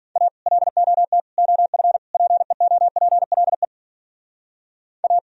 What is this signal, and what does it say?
Morse code, heard as a keyed tone, about 51 wpm, 700 Hz. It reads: A LOT OF PEOPLE A